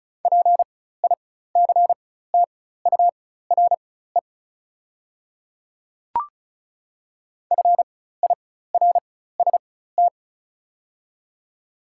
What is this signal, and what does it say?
Morse code, heard as a keyed tone, about 36 wpm, spaced as for 16 wpm, 690 Hz. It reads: PICTURE FIRST
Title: 4x Character Spacing - Top 200 Words - 35wpm